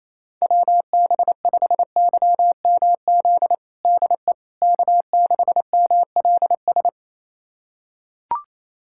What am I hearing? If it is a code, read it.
Morse code, heard as a keyed tone, about 28 wpm, 700 Hz. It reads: WB5YMZ DE K6MLS
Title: USA Callsign DE Callsign 28wpm